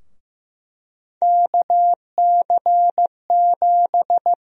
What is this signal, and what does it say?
Morse code, heard as a keyed tone, about 15 wpm, 700 Hz. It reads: KC7
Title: USA Call Sign Prefixes - 15wpm